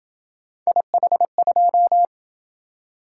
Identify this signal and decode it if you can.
Morse code, heard as a keyed tone, about 26 wpm, 700 Hz. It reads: IH2